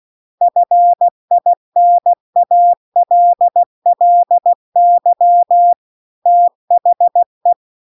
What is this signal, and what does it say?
Morse code, heard as a keyed tone, about 16 wpm, 700 Hz. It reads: FINALLY THE